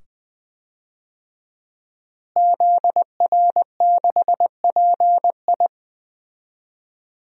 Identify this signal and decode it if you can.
Morse code, heard as a keyed tone, about 21 wpm, 700 Hz. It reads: ZR6PI
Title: DXCC-100 Call Signs - 5 Characters - Repeated 2x - 20wpm